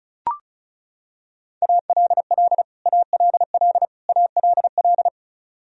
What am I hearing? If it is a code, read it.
Morse code, heard as a keyed tone, about 34 wpm, 700 Hz. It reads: ALL ALL ALL
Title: Top 100 Words Repeated 35wpm